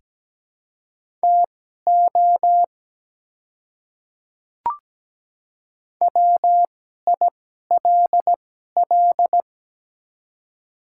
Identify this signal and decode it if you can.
Morse code, heard as a keyed tone, about 17 wpm, spaced as for 12 wpm, 700 Hz. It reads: TO WILL
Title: 2x Character Spacing - Top 100 Words - 17wpm